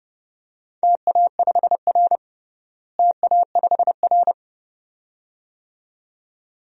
Morse code, heard as a keyed tone, about 32 wpm, 700 Hz. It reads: TA5R TA5R
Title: DXCC-100 Call Signs - 4 Characters - Repeated 2x - 30wpm